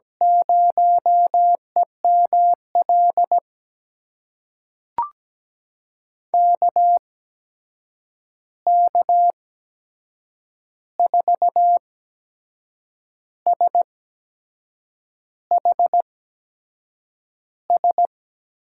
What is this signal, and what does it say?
Morse code, heard as a keyed tone, about 17 wpm, spaced as for 6 wpm, 700 Hz. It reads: 0EML KK4SHS
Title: USA Call Signs - 6 Characters - 8x Character Spacing - 17wpm